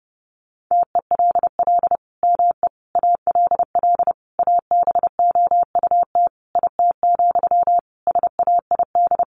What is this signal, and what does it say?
Morse code, heard as a keyed tone, about 28 wpm, 700 Hz. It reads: TELL ME ALL ABOUT IT, SAID
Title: Sentences from Top 1500 Words 30wpm